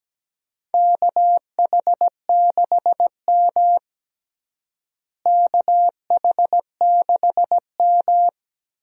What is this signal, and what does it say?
Morse code, heard as a keyed tone, about 17 wpm, 700 Hz. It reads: KH6M KH6M
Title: DXCC-100 Call Signs - 4 Characters - Repeated 2x - 17wpm